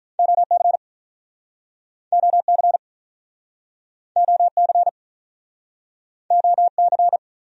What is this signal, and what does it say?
Morse code, heard as a keyed tone, about 46 wpm, 700 Hz. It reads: OC OC OC OC